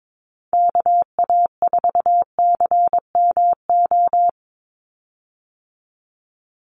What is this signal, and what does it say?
Morse code, heard as a keyed tone, about 22 wpm, 700 Hz. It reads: KA4CMO
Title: USA Call Signs 22wpm